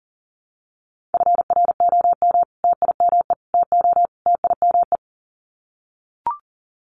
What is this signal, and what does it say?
Morse code, heard as a keyed tone, about 40 wpm, 700 Hz. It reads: FROM TIME TO TIME
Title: Sets of 4 Words 40wpm